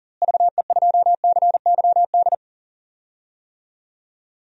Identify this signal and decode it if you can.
Morse code, heard as a keyed tone, about 38 wpm, 700 Hz. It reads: VE2CYD
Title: DXCC-100 Call Signs - 6 Characters - Repeated 2x - 40wpm